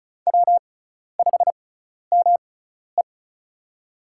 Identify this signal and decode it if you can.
Morse code, heard as a keyed tone, about 34 wpm, spaced as for 12 wpm, 700 Hz. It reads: W5ME
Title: USA Call Signs - 4 Characters - 6x Character Spacing - 35wpm